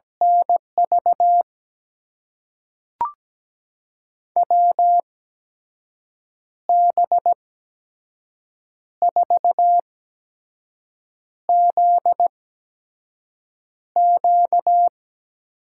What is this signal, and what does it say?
Morse code, heard as a keyed tone, about 17 wpm, spaced as for 5 wpm, 700 Hz. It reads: NV WB4ZQ
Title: USA Call Signs - 6 Characters - 8x Character Spacing - 17wpm